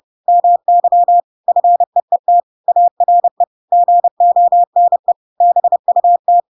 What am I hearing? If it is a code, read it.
Morse code, heard as a keyed tone, about 30 wpm, 700 Hz. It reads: MY FEET ARE GONE BUT
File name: Pinocchio / Pinocchio ch7 30 30 ik1hge